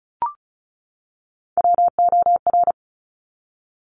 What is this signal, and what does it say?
Morse code, heard as a keyed tone, about 36 wpm, 700 Hz. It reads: WOR
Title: Top Three Letter Combinations in Words 35wpm